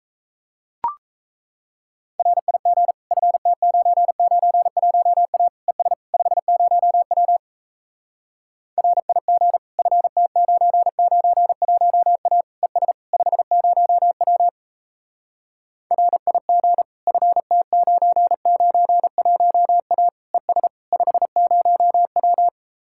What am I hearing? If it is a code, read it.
Morse code, heard as a keyed tone, about 37 wpm, 700 Hz. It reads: RIG FT991A ES 50W RIG FT991A ES 50W RIG FT991A ES 50W